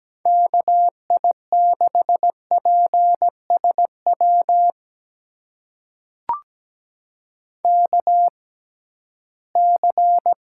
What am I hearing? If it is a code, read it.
Morse code, heard as a keyed tone, about 17 wpm, spaced as for 13 wpm, 700 Hz. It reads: KI6PSW KC
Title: USA Call Signs - 6 Characters - 6x Character Spacing - 17wpm